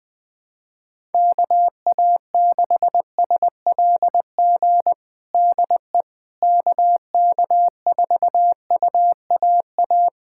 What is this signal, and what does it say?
Morse code, heard as a keyed tone, about 21 wpm, 700 Hz. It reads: KA6SLG DE KK4UAA
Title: USA Callsign DE Callsign - 6 Characters - 20wpm